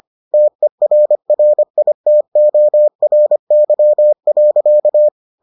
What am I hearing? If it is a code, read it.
Morse code, heard as a keyed tone, about 25 wpm, 600 Hz. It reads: TERRITORY.